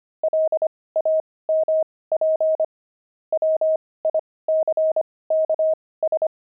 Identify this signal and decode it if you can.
Morse code, heard as a keyed tone, about 25 wpm, spaced as for 18 wpm, 630 Hz. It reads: LAMP WICKS
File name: pangram-fs-18-cs-25-nine